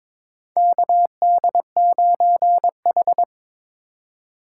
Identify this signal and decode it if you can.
Morse code, heard as a keyed tone, about 21 wpm, 700 Hz. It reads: KD9H
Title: USA Call Signs - 4 Characters - 4x Character Spacing - 22wpm